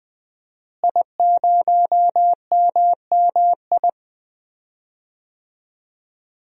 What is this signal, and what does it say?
Morse code, heard as a keyed tone, about 20 wpm, 700 Hz. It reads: I0MMI
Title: DXCC-100 Call Signs - 5 Characters - Repeated 3x - 20wpm